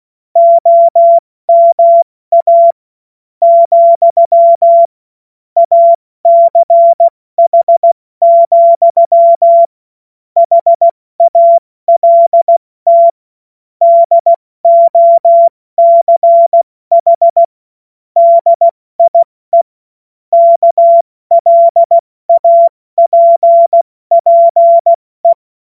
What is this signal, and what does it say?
Morse code, heard as a keyed tone, about 16 wpm, spaced as for 14 wpm, 680 Hz. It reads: OMA, ACH, HALT DOCH DIE KLAPPE
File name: Morsekurs / Witze / Witz 06. Juli 2021, 70 F80